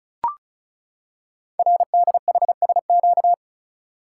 Morse code, heard as a keyed tone, about 36 wpm, 700 Hz. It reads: RDHSQ
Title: Code Groups - 5 Letters - 35wpm